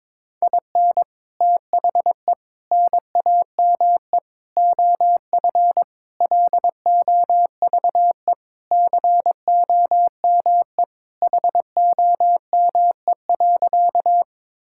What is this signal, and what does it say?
Morse code, heard as a keyed tone, about 22 wpm, 700 Hz. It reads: IN THE NAME OF LOVE COME HOME.
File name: practice / 22wpm / Sentences with 3-Word Phrases - 22wpm